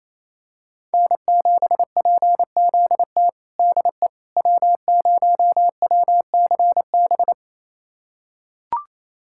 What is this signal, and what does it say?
Morse code, heard as a keyed tone, about 28 wpm, 700 Hz. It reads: N7PZT DE W0WCB
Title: USA Callsign DE Callsign 28wpm